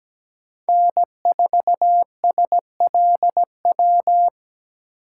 Morse code, heard as a keyed tone, about 17 wpm, 700 Hz. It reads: N4SLW
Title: USA Call Signs - 5 Characters - 4x Character Spacing - 17wpm